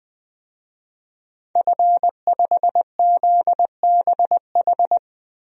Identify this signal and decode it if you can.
Morse code, heard as a keyed tone, about 21 wpm, 700 Hz. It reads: F5ZBH
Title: DXCC-100 Call Signs - 5 Characters - Repeated 2x - 20wpm